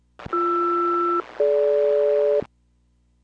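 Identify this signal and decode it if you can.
Aircraft SELCAL, heard as two pairs of tones. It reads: BR-DG
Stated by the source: Aircraft SELCAL received at Paine Field.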